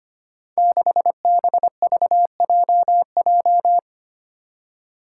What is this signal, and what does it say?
Morse code, heard as a keyed tone, about 25 wpm, 700 Hz. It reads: 6BVJJ
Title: Five Challenging Letter-Number Combinations 25wpm